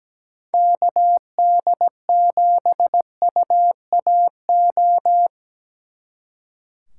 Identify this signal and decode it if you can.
Morse code, heard as a keyed tone, about 19 wpm, 700 Hz. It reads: KD7UAO
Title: Speed Racing - USA 2x3 Call Signs 17wpm